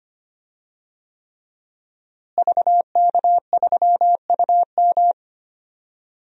Morse code, heard as a keyed tone, about 25 wpm, 700 Hz. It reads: VK3UM
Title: DXCC-100 Call Signs - 5 Characters - Repeated 3x - 25wpm